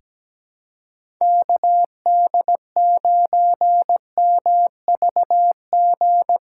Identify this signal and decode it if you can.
Morse code, heard as a keyed tone, about 17 wpm, 700 Hz. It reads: KD9MVG
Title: USA Callsign DE Callsign 17wpm